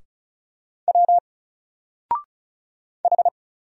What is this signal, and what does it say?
Morse code, heard as a keyed tone, about 36 wpm, 700 Hz. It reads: W H
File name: Single-Letter-Number-35wpm